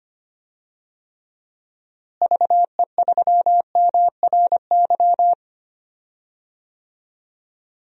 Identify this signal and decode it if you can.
Morse code, heard as a keyed tone, about 25 wpm, 700 Hz. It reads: VE3MRY
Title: DXCC-100 Call Signs - 6 Characters - Repeated 3x - 25wpm